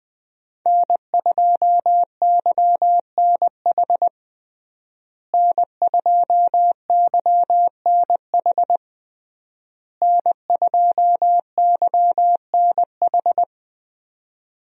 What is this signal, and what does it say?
Morse code, heard as a keyed tone, about 21 wpm, 700 Hz. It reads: N2YNH N2YNH N2YNH
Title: USA Call Signs - 5 Characters - Repeated 3x - 20wpm